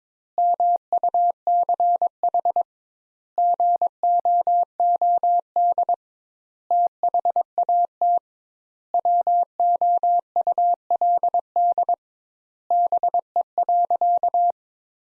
Morse code, heard as a keyed tone, about 22 wpm, 700 Hz. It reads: MUCH GOOD THAT WOULD BE.
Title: Sentences Top 200 Words - 22wpm